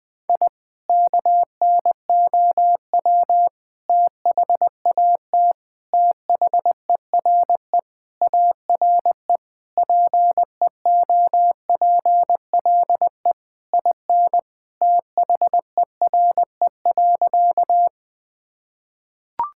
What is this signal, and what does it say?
Morse code, heard as a keyed tone, about 21 wpm, 700 Hz. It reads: I KNOW THAT THERE ARE PEOPLE IN THERE.
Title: Sentences with 3-Word Phrases - 20wpm